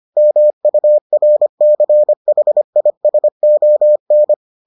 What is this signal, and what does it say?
Morse code, heard as a keyed tone, about 25 wpm, 600 Hz. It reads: MURCHISON